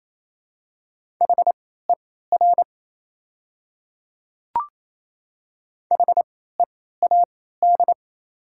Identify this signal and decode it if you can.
Morse code, heard as a keyed tone, about 27 wpm, spaced as for 16 wpm, 700 Hz. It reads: HER HEAD
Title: 3x Character Spacing - Top 300 Words - 28wpm